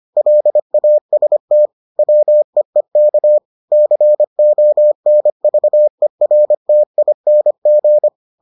Morse code, heard as a keyed tone, about 25 wpm, 600 Hz. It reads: LAST WEEK CONVERTING